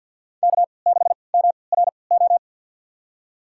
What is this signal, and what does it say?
Morse code, heard as a keyed tone, about 46 wpm, spaced as for 27 wpm, 700 Hz. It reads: K6MRO
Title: USA Call Signs - 5 Characters - 3x Character Spacing - 50wpm